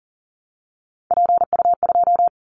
Morse code, heard as a keyed tone, about 42 wpm, 700 Hz. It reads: PU2
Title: Callsign Prefixes for Top 50 DXCC - 40wpm